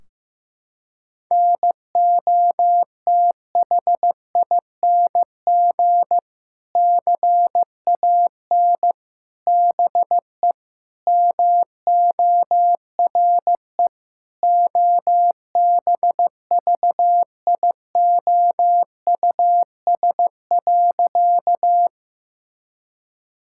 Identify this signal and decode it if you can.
Morse code, heard as a keyed tone, about 16 wpm, 700 Hz. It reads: NOTHING CAN BE MORE OBVIOUS.
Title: Sentences from Top 1600 Words 15wpm